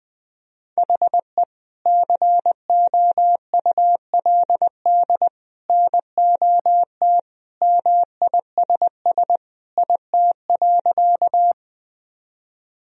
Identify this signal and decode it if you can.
Morse code, heard as a keyed tone, about 20 wpm, 700 Hz. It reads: HE COULD NOT MISS IT.
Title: Sentences from Top 1000 Words - Review Part 3 - 20wpm